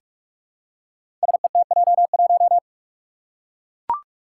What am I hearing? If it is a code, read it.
Morse code, heard as a keyed tone, about 44 wpm, 700 Hz. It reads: SETJ1
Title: Five Challenging Letter-Number Combinations 45wpm